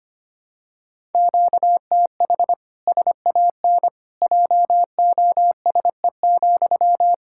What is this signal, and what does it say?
Morse code, heard as a keyed tone, about 25 wpm, 700 Hz. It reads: QTH SAN JOSE,
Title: Practice QSO - Exchange 1 - Beginning - 25wpm